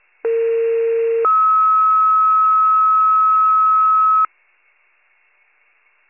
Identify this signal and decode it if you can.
Two-tone page, heard as a one-second tone, then a three-second tone, about 460 then 1290 Hz. It is heard from 0.2 to 4.3 s.